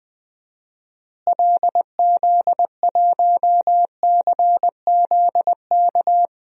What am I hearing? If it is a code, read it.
Morse code, heard as a keyed tone, about 21 wpm, 700 Hz. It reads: LZ1CZK
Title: DXCC-100 Call Signs - 6 Characters - Repeated 3x - 20wpm